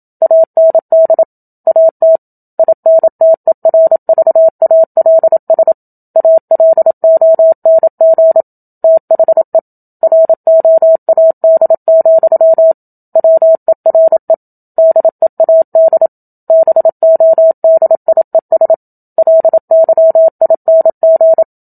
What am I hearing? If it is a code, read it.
Morse code, heard as a keyed tone, about 27 wpm, 650 Hz. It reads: AND AT INTERVALS ALONG THE ROAD, WERE DEAD BODIES LYING
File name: Book Two / 01-Chapter 1 - 27 WPM